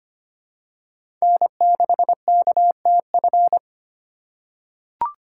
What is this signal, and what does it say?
Morse code, heard as a keyed tone, about 25 wpm, 700 Hz. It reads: N6KTF
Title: USA Call Signs - 5 Characters - 3x Character Spacing - 25wpm